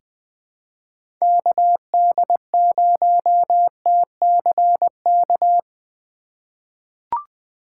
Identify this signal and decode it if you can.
Morse code, heard as a keyed tone, about 21 wpm, 700 Hz. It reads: KD0TCK
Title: USA Call Signs - 4x Character Spacing - 20wpm